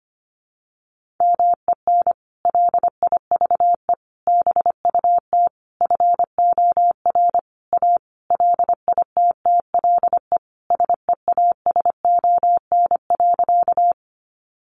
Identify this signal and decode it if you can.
Morse code, heard as a keyed tone, about 25 wpm, 700 Hz. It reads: MEN LIVE BUT FOR A LITTLE SEASON.
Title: Sentences from Top 1400 Words 25wpm